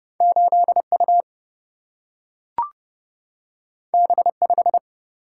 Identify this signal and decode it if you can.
Morse code, heard as a keyed tone, about 30 wpm, 700 Hz. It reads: 8 U B 5
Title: Single-Letter Single-Number 30wpm